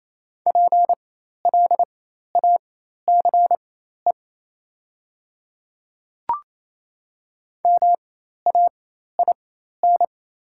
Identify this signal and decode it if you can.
Morse code, heard as a keyed tone, about 27 wpm, spaced as for 13 wpm, 700 Hz. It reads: PLACE MAIN
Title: 4x Character Spacing - Top 300 Words - 28wpm